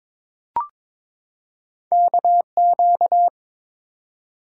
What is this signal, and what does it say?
Morse code, heard as a keyed tone, about 23 wpm, 700 Hz. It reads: KQ
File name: practice / 22wpm / Two Letter-Number Combinations - 22wpm